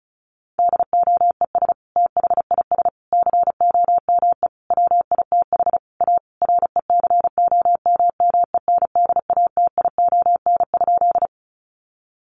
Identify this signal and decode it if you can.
Morse code, heard as a keyed tone, about 34 wpm, 700 Hz. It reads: DOES THIS COME WITH A RECOMMENDATION?
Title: Sentences from Top 1400 Words 35wpm